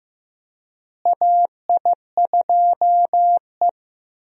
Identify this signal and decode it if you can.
Morse code, heard as a keyed tone, about 15 wpm, 700 Hz. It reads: AI2E
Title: USA Call Signs - 4 Characters - Repeated 3x - 15wpm